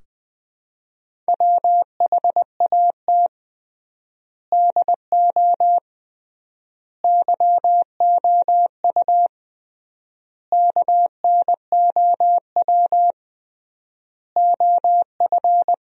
Morse code, heard as a keyed tone, about 20 wpm, 700 Hz. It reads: WHAT DO YOU KNOW OF
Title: 3x Word-Spacing - Sentences from Top 100 Words - 20wpm